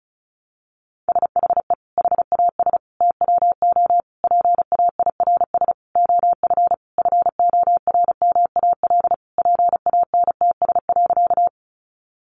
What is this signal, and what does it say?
Morse code, heard as a keyed tone, about 34 wpm, 700 Hz. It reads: SHE HAS TWO PAIRS OF FORMAL PANTS.
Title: Additional Easy Phrases 35wpm